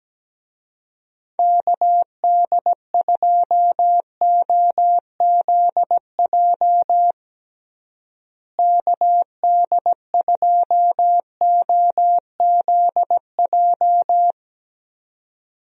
Morse code, heard as a keyed tone, about 17 wpm, 700 Hz. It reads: KD2OZJ KD2OZJ
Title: USA Call Signs - 6 Characters - Repeated 2x - 17wpm